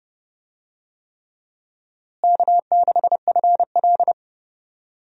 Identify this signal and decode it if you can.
Morse code, heard as a keyed tone, about 31 wpm, 700 Hz. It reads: K6FL
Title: USA Call Signs - 4 Characters - Repeated 3x - 30wpm